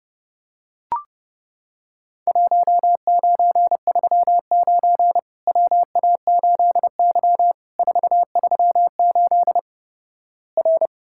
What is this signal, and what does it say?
Morse code, heard as a keyed tone, about 32 wpm, 700 Hz. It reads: 1939 WA8Y 438 R
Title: Worked All Europe DX Contest - QTC Exchange - 30wpm